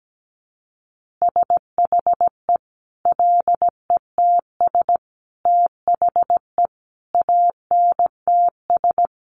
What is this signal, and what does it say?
Morse code, heard as a keyed tone, about 17 wpm, 700 Hz. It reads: SHE LETS THE ANTS
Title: Easy Phrases 17wpm